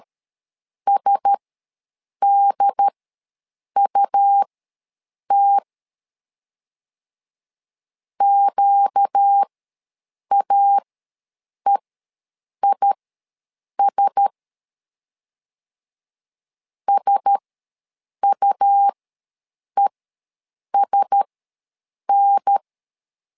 Morse code, heard as a keyed tone, about 13 wpm, spaced as for 7 wpm, 800 Hz. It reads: SDUT QAEIS SUESN